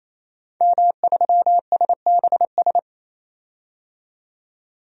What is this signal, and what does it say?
Morse code, heard as a keyed tone, about 28 wpm, 700 Hz. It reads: M3SBS
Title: DXCC-100 Call Signs - 5 Characters - 28wpm